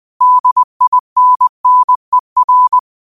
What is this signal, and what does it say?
Morse code, heard as a keyed tone, about 20 wpm, 1000 Hz. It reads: DINNER